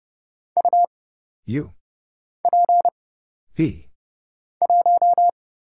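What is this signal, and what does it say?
Morse code, heard as a keyed tone, about 30 wpm, 700 Hz. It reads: U P 1
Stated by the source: Morse Code Ninja single-character practice (letters, numbers, punctuation, prosigns) at 30 wpm, by Kurt Zoglmann (AD0WE).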